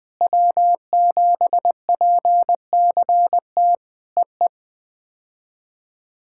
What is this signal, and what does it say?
Morse code, heard as a keyed tone, about 19 wpm, 700 Hz. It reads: W7PCT EE
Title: POTA QSO Practice - 20wpm